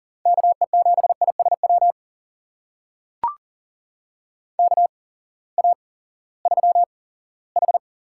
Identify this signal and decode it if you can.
Morse code, heard as a keyed tone, about 40 wpm, spaced as for 20 wpm, 700 Hz. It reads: KE7ISW KA3H
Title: USA Call Signs - 6 Characters - 8x Character Spacing - 40wpm